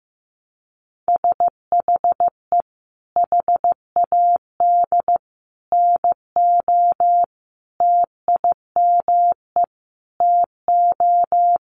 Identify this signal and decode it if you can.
Morse code, heard as a keyed tone, about 15 wpm, 700 Hz. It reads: SHE HAD NO TIME TO
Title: Sentences from Top 1100 Words 15wpm